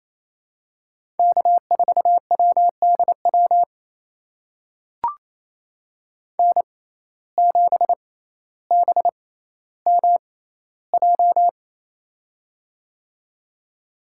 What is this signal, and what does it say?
Morse code, heard as a keyed tone, about 28 wpm, spaced as for 14 wpm, 700 Hz. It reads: K4WDW N7BMJ
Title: USA Call Signs - 5 Characters - 6x Character Spacing - 28wpm